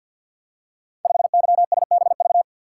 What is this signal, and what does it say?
Morse code, heard as a keyed tone, about 47 wpm, 700 Hz. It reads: 5YSBV
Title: Code Groups - 5 Letter:Numbers - 50wpm